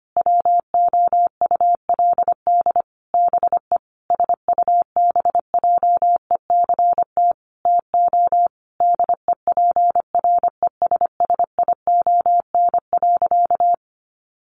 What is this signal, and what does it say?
Morse code, heard as a keyed tone, about 25 wpm, 700 Hz. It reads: WOULD BE SUBJECT TO DEPRESSION.
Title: Sentences from Top 1300 Words 25wpm